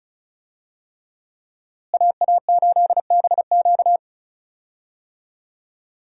Morse code, heard as a keyed tone, about 36 wpm, 700 Hz. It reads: AA8BQ
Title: USA Call Signs - 5 Characters - Repeated 3x - 35wpm